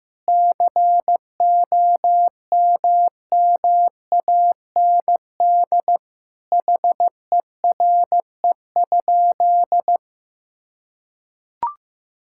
Morse code, heard as a keyed tone, about 15 wpm, 700 Hz. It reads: COMMAND HERE?
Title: Sentences from Top 1600 Words 15wpm